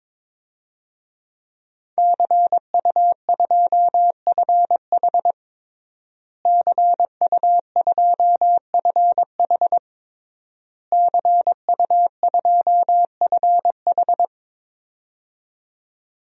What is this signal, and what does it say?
Morse code, heard as a keyed tone, about 22 wpm, 700 Hz. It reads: CU2FH CU2FH CU2FH
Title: DXCC-100 Call Signs - 5 Characters - Repeated 3x - 22wpm